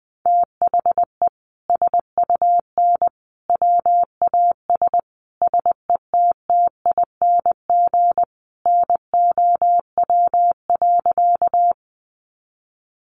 Morse code, heard as a keyed tone, about 20 wpm, 700 Hz. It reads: THE SUN WAS SETTING NOW.
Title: Sentences from Top 1300 Words 20wpm